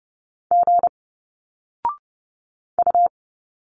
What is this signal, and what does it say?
Morse code, heard as a keyed tone, about 30 wpm, 700 Hz. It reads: G U
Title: Single Letters 30wpm